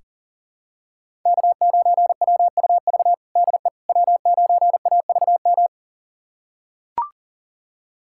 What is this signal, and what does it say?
Morse code, heard as a keyed tone, about 40 wpm, 700 Hz. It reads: K9WUV DE W9AVM
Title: USA Callsign DE Callsign - 5 Characters - 40wpm